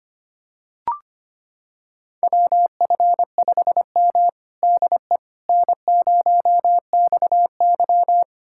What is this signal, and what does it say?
Morse code, heard as a keyed tone, about 25 wpm, 700 Hz. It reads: WF5M DE N0XY
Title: USA Callsign DE Callsign - 4 Characters - 25wpm